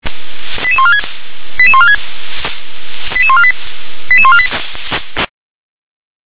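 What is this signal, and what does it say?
Five-tone selective call (ZVEI), heard as quick runs of five tones, tones of about 70 ms. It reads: 88146 88146 88148 88148
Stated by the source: Five-tone ZVEI selective call received on SDR.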